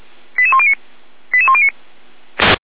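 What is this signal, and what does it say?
Five-tone selective call (ZVEI), heard as quick runs of five tones, tones of about 80 ms. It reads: 88109 88109
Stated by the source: ZVEI five-tone call 88469 received by SDR, recorded by Feuerwehr Orlamünde.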